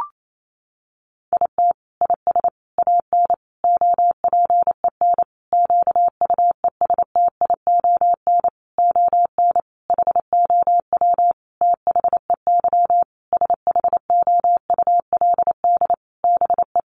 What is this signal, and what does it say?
Morse code, heard as a keyed tone, about 28 wpm, 700 Hz. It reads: IT IS AN OPEN QUESTION ON HOW THEY SHOULD BE
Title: Sentences from Top 1300 Words 28wpm